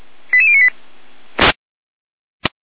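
Five-tone selective call (ZVEI), heard as quick runs of five tones, tones of about 70 ms. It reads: 88098
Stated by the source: ZVEI five-tone call received on an SDR.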